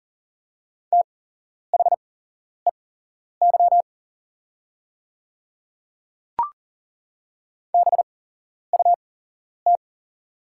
Morse code, heard as a keyed tone, about 40 wpm, spaced as for 11 wpm, 700 Hz. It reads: THEY BUT
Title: 8x Character Spacing - Top 100 Words - 40wpm